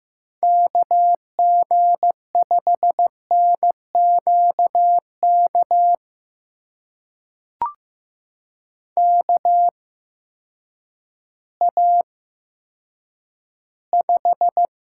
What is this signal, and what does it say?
Morse code, heard as a keyed tone, about 15 wpm, spaced as for 9 wpm, 700 Hz. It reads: KG5NQK KA5
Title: USA Call Signs - 6 Characters - 8x Character Spacing - 15wpm